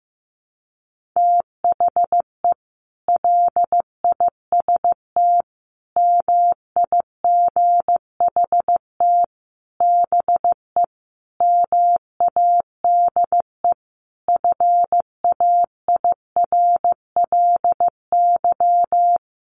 Morse code, heard as a keyed tone, about 15 wpm, 700 Hz. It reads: THE LIST MIGHT BE MADE FAIRLY
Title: Sentences from Top 1100 Words 15wpm